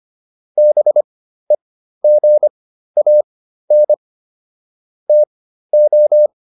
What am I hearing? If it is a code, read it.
Morse code, heard as a keyed tone, about 25 wpm, spaced as for 13 wpm, 600 Hz. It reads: BEGAN TO